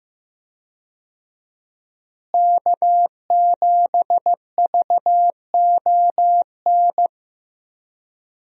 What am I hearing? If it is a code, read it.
Morse code, heard as a keyed tone, about 15 wpm, 700 Hz. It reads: K7VON
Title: USA Call Signs - 5 Characters - Repeated 2x - 15wpm